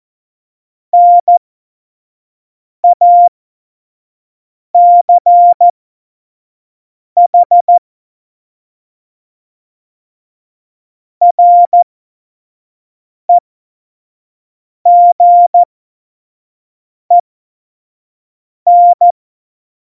Morse code, heard as a keyed tone, about 14 wpm, spaced as for 5 wpm, 700 Hz. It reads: NACH REGEN
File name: Morsekurs / Witze / Witz 22. Oktober 2019, 25 F70